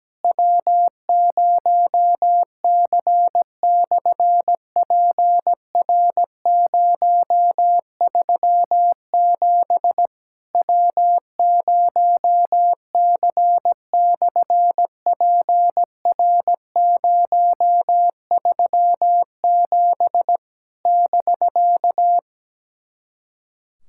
Morse code, heard as a keyed tone, about 17 wpm, 700 Hz. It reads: W0C/PR037 W0C/PR037 <BK>
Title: SOTA QSO Practice 17wpm